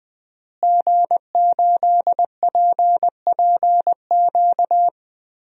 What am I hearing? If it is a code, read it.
Morse code, heard as a keyed tone, about 21 wpm, 700 Hz. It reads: G8PPQ
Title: DXCC-100 Call Signs - 5 Characters - Repeated 3x - 20wpm